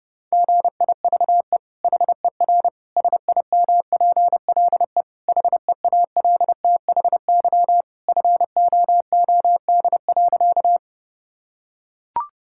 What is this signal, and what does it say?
Morse code, heard as a keyed tone, about 30 wpm, 700 Hz. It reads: GIVE HER SIMPLE HEALTHY FOOD.
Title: Sentences from Top 1000 Words - Review Part 5 - 30wpm